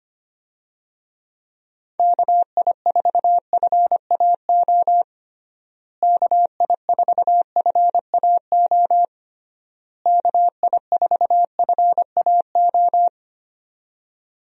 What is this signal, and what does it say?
Morse code, heard as a keyed tone, about 25 wpm, 700 Hz. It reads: KI4FAO KI4FAO KI4FAO
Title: USA Call Signs - 6 Characters - Repeated 3x - 25wpm